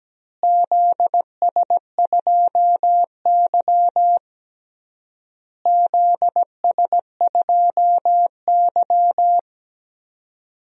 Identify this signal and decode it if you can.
Morse code, heard as a keyed tone, about 17 wpm, 700 Hz. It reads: ZS2Y ZS2Y
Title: DXCC-100 Call Signs - 4 Characters - Repeated 3x - 17wpm